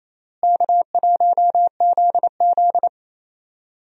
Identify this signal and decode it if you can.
Morse code, heard as a keyed tone, about 28 wpm, 700 Hz. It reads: K1ZZ
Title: USA Call Signs - 4 Characters - Repeated 3x - 28wpm